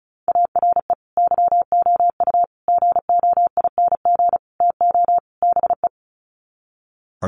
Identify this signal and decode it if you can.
Morse code, heard as a keyed tone, about 34 wpm, 700 Hz. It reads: ARE YOU GOING TO BE
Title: Sets of 5 Words Encore 35wpm